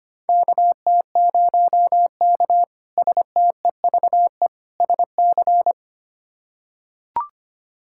Morse code, heard as a keyed tone, about 25 wpm, 700 Hz. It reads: KT0K STEVE SC
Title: K1USN SST QSO Practice - 25wpm